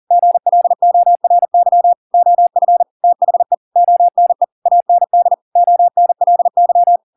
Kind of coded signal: Morse code